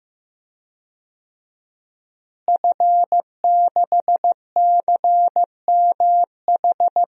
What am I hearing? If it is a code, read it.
Morse code, heard as a keyed tone, about 15 wpm, 700 Hz. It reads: F6CMH
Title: DXCC-100 Call Signs - 5 Characters - Repeated 3x - 15wpm